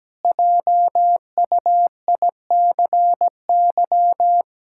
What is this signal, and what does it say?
Morse code, heard as a keyed tone, about 17 wpm, 700 Hz. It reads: JUICY
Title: Pangram Challenge v2 - 17wpm